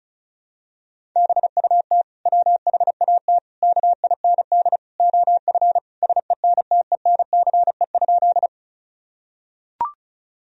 Morse code, heard as a keyed tone, about 36 wpm, 700 Hz. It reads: BUT WHAT KIND OF SENTENCE?
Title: Sentences from Top 1000 Words - Review Part 1 - 35wpm